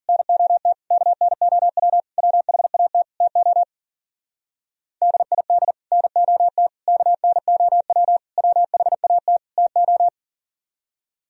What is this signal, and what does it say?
Morse code, heard as a keyed tone, about 52 wpm, 700 Hz. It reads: NOT KNOW WHAT TO DID NOT KNOW WHAT TO